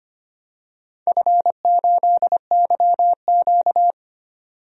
Morse code, heard as a keyed tone, about 25 wpm, 700 Hz. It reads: F8YQ
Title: DXCC-100 Call Signs - 4 Characters - 25wpm